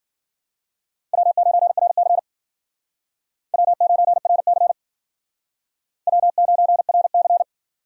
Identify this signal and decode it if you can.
Morse code, heard as a keyed tone, about 51 wpm, 700 Hz. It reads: W9RC W9RC W9RC